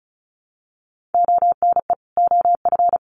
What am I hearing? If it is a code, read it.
Morse code, heard as a keyed tone, about 34 wpm, 700 Hz. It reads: ONE OF
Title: Sets of 2 Words 35wpm